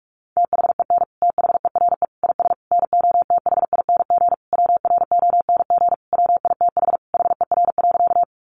Morse code, heard as a keyed tone, about 44 wpm, 700 Hz. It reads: THEN THERE IS NOTHING WRONG WITH HER.
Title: Sentences from Top 900 Words 45wpm